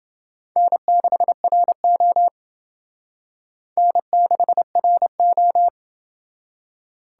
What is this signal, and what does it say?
Morse code, heard as a keyed tone, about 28 wpm, 700 Hz. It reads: N6RO N6RO